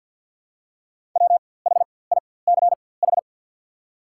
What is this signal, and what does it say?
Morse code, heard as a keyed tone, about 54 wpm, spaced as for 24 wpm, 690 Hz. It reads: WHICH
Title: 4x Character Spacing - Top 100 Words - 50wpm